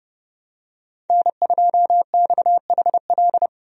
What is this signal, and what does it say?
Morse code, heard as a keyed tone, about 30 wpm, 700 Hz. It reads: N2XHL
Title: USA Call Signs - 5 Characters - Repeated 2x - 30wpm